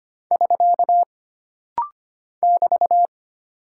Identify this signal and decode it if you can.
Morse code, heard as a keyed tone, about 25 wpm, 700 Hz. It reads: <SK> <BT>
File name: Single Punctuation-Prosign - Rapid-Fire - 25wpm